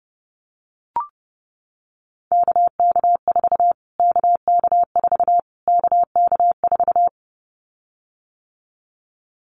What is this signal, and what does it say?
Morse code, heard as a keyed tone, about 30 wpm, 700 Hz. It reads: KK4 KK4 KK4
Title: Callsign Prefixes for Top 25 DXCC - Repeated 3x - 30wpm